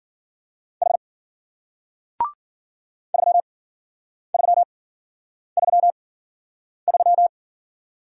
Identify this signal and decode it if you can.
Morse code, heard as a keyed tone, about 53 wpm, spaced as for 9 wpm, 700 Hz. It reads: H 3333